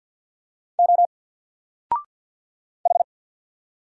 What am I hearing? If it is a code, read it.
Morse code, heard as a keyed tone, about 48 wpm, 700 Hz. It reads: O H
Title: Single Letter-Number Alt 50wpm